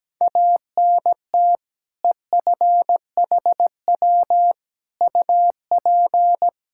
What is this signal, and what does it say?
Morse code, heard as a keyed tone, about 17 wpm, 700 Hz. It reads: ANT EFHW UP
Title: Practice QSO - Exchange 2 - Info - 17wpm